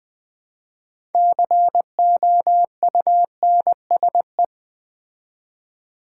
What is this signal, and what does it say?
Morse code, heard as a keyed tone, about 20 wpm, 700 Hz. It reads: COUNSE
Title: Build A Word - 3 Letters At A Time - 9-Letter Words - 20wpm